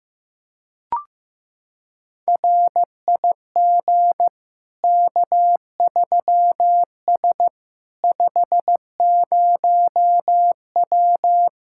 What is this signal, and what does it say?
Morse code, heard as a keyed tone, about 15 wpm, 700 Hz. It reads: RIG K3S 50W
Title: Practice QSO - Exchange 2 - Info - 15wpm